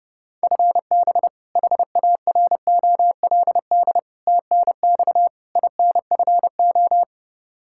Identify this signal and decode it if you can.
Morse code, heard as a keyed tone, about 30 wpm, 700 Hz. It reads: FB HAROLD TNX INFO
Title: Practice QSO - Exchange 3 - More Info - 30wpm